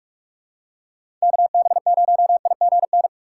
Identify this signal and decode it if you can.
Morse code, heard as a keyed tone, about 47 wpm, 700 Hz. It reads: KB0IGN
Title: USA Call Signs - 6 Characters - 4x Character Spacing - 45wpm